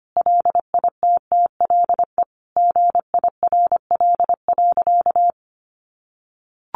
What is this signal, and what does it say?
Morse code, heard as a keyed tone, about 25 wpm, 700 Hz. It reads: LITTLE GIRL.
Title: Sentences from Top 400 Words 25wpm